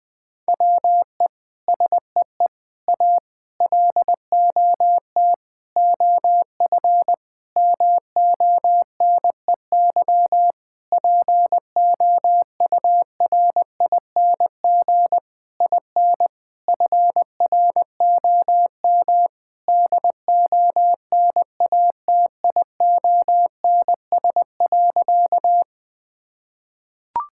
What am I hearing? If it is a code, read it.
Morse code, heard as a keyed tone, about 21 wpm, 700 Hz. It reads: WE SEE A LOT OF MONEY POURING IN FROM DONATIONS.
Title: Sentences with 9-Word Phrases - 20wpm